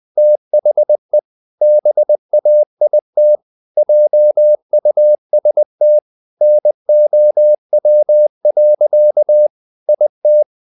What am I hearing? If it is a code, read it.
Morse code, heard as a keyed tone, about 20 wpm, 600 Hz. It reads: THE BAIT JUST NOW. IT
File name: the old man and the sea / the old man and the sea part 1 20wpm